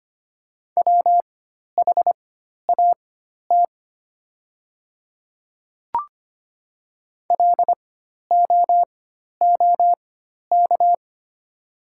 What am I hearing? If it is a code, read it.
Morse code, heard as a keyed tone, about 25 wpm, spaced as for 12 wpm, 700 Hz. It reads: WHAT LOOK
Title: 4x Character Spacing - Top 200 Words - 25wpm